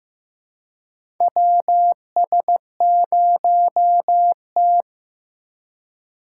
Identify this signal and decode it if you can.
Morse code, heard as a keyed tone, about 15 wpm, 700 Hz. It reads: WS0T
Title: USA Call Signs - 4 Characters - 3x Character Spacing - 15wpm